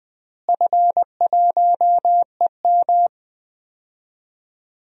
Morse code, heard as a keyed tone, about 20 wpm, 700 Hz. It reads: F1EM
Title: DXCC-100 Call Signs - 4 Characters - Repeated 3x - 20wpm